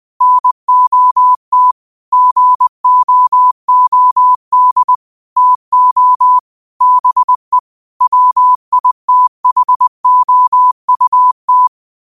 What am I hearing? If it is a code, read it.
Morse code, heard as a keyed tone, about 20 wpm, 1000 Hz. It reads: NOT GOOD TO BE WITHOUT
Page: New quotes every day in morse code at 20 Words per minute.